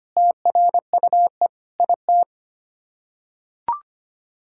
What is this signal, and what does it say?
Morse code, heard as a keyed tone, about 25 wpm, 700 Hz. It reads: TRUE IT
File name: Sets of 9 Words - 25wpm